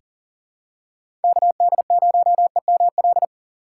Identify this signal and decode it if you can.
Morse code, heard as a keyed tone, about 40 wpm, 700 Hz. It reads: KD0EML
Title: USA Call Signs - 6 Characters - 6x Character Spacing - 40wpm